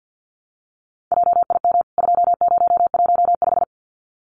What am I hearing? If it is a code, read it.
Morse code, heard as a keyed tone, about 50 wpm, 700 Hz. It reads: JIM 2015
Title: CWT Exchange 50wpm